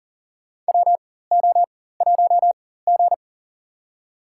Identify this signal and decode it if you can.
Morse code, heard as a keyed tone, about 39 wpm, spaced as for 19 wpm, 700 Hz. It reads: WO1G
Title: USA Call Signs - 4 Characters - 4x Character Spacing - 40wpm